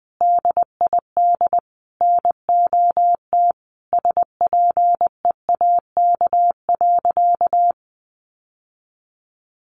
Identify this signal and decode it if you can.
Morse code, heard as a keyed tone, about 20 wpm, 700 Hz. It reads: DID NOT SPEAK.
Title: Sentences from Top 1500 Words 20wpm